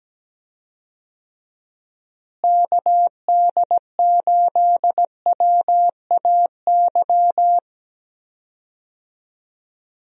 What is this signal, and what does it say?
Morse code, heard as a keyed tone, about 17 wpm, 700 Hz. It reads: KD8WAY
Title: USA Call Signs - 6 Characters - Repeated 2x - 17wpm